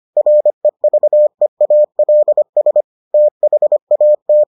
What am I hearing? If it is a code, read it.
Morse code, heard as a keyed tone, about 25 wpm, 600 Hz. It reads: REVEALS THAT